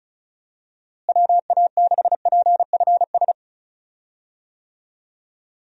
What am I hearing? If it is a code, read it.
Morse code, heard as a keyed tone, about 36 wpm, 700 Hz. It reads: WA6PFS
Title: USA Call Signs - 6 Characters - Repeated 2x - 35wpm